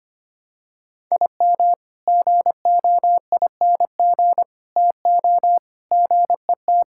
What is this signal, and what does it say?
Morse code, heard as a keyed tone, about 25 wpm, 700 Hz. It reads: IM GOING TO GET
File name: practice / 25wpm / Sentences with 3-Word Phrases - 25wpm